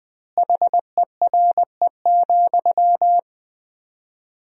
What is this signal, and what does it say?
Morse code, heard as a keyed tone, about 20 wpm, 700 Hz. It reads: HERE,
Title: 4x Word-Spacing - Sentences from Top 300 Words - 20wpm